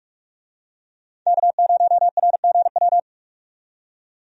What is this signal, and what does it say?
Morse code, heard as a keyed tone, about 44 wpm, 700 Hz. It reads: K0RGW
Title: USA Call Signs - 5 Characters - 45wpm